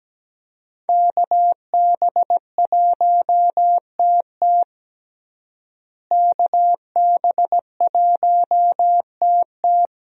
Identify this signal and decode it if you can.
Morse code, heard as a keyed tone, about 17 wpm, 700 Hz. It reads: KB1TT KB1TT
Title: USA Call Signs - 5 Characters - Repeated 2x - 17wpm